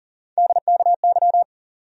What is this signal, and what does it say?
Morse code, heard as a keyed tone, about 40 wpm, 700 Hz. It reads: DKY
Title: USA Call Sign Postfixes - 3 Letters 40wpm